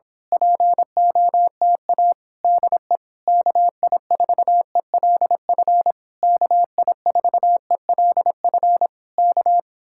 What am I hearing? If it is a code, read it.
Morse code, heard as a keyed tone, about 26 wpm, 700 Hz. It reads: POTA DE KI4ELF KI4ELF K
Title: Speed-Racing Common QSO Phrases - 22wpm